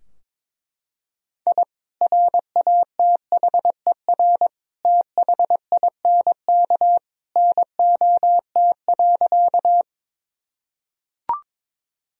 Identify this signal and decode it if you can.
Morse code, heard as a keyed tone, about 22 wpm, 700 Hz. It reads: I RATHER THINK NOT.
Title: Sentences from Top 1000 Words - Review Part 4 - 22wpm